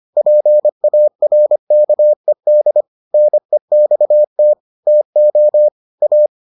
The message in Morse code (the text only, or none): PARKED NEXT TO A